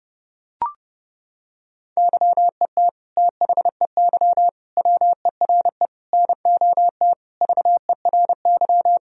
Morse code, heard as a keyed tone, about 30 wpm, 700 Hz. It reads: YET THEY WERE NOT VERY
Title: Sentences from Top 1000 Words - Review Part 3 - 30wpm